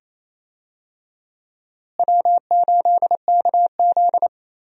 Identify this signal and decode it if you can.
Morse code, heard as a keyed tone, about 28 wpm, 700 Hz. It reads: W8KZ
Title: USA Call Signs - 4 Characters - Repeated 2x - 28wpm